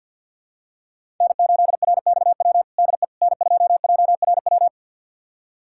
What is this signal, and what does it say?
Morse code, heard as a keyed tone, about 46 wpm, 700 Hz. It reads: N8RXW DE N2JRW
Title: USA Callsign DE Callsign - 5 Characters - 50wpm